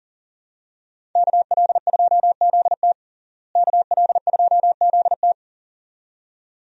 Morse code, heard as a keyed tone, about 40 wpm, 700 Hz. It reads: KL2ZT KL2ZT
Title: USA Call Signs - 5 Characters - Repeated 2x - 40wpm